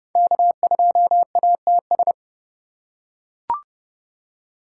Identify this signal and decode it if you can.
Morse code, heard as a keyed tone, about 30 wpm, 700 Hz. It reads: K2ATS
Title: USA Call Signs - 5 Characters - 4x Character Spacing - 30wpm